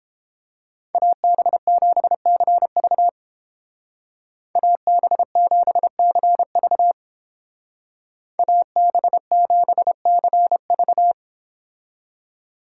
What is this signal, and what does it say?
Morse code, heard as a keyed tone, about 29 wpm, 700 Hz. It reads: AB7CV AB7CV AB7CV